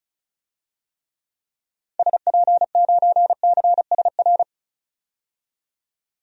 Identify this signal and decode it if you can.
Morse code, heard as a keyed tone, about 36 wpm, 700 Hz. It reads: SP9CSR
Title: DXCC-100 Call Signs - 6 Characters - 35wpm